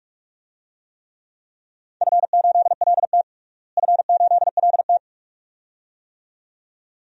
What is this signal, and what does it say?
Morse code, heard as a keyed tone, about 46 wpm, 700 Hz. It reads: F8LT F8LT
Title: DXCC-100 Call Signs - 4 Characters - Repeated 2x - 45wpm